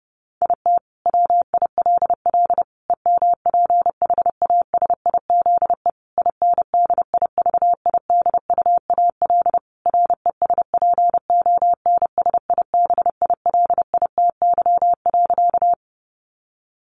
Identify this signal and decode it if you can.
Morse code, heard as a keyed tone, about 28 wpm, 700 Hz. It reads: IT WILL EMPHASIZE INDIVIDUAL RESPONSIBILITY.
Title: Sentences from Top 1500 Words 30wpm